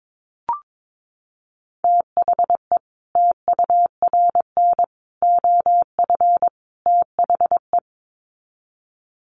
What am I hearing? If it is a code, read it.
Morse code, heard as a keyed tone, about 22 wpm, 700 Hz. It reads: THE TURN OF THE
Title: Sets of 4 Words 22wpm